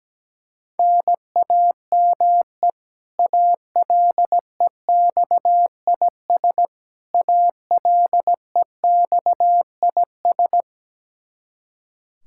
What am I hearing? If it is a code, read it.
Morse code, heard as a keyed tone, about 17 wpm, 700 Hz. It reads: NAME ALEXIS ALEXIS
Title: K1USN SST QSO Practice 17wpm